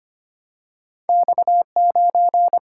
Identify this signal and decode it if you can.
Morse code, heard as a keyed tone, about 25 wpm, 700 Hz. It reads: X 9
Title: Single-Letter Single-Number 25wpm